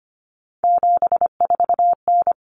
Morse code, heard as a keyed tone, about 25 wpm, 700 Hz. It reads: 74N
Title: Common Cut Numbers - 3-Digit 25wpm